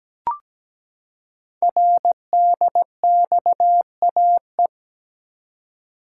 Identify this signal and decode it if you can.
Morse code, heard as a keyed tone, about 17 wpm, 700 Hz. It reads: RDXAE
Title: Code Groups - 5 Letters - 17wpm